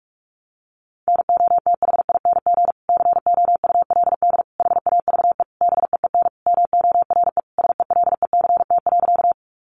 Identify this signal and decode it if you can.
Morse code, heard as a keyed tone, about 44 wpm, 700 Hz. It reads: NOTHING COULD HAVE BEEN MORE SELECT.
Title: Sentences from Top 1300 Words 45wpm